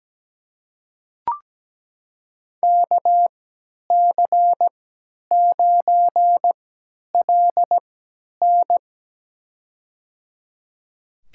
Morse code, heard as a keyed tone, about 17 wpm, spaced as for 10 wpm, 700 Hz. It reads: KC9LN
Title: USA Call Signs - 5 Characters - 3x Character Spacing - 17wpm